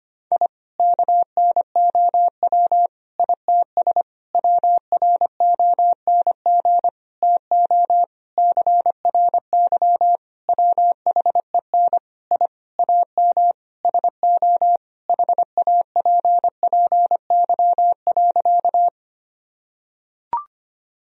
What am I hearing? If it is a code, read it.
Morse code, heard as a keyed tone, about 25 wpm, 700 Hz. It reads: I KNOW ITS WRONG TO CRY WHEN I AM SO HAPPY.
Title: Sentences from Top 1000 Words - Review Part 2 - 25wpm